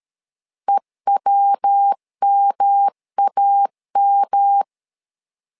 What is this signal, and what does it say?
Morse code, heard as a keyed tone, about 13 wpm, 800 Hz. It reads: EWMAM
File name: LEZIONI / 3 MWRO / 3 MW 12wpm 1633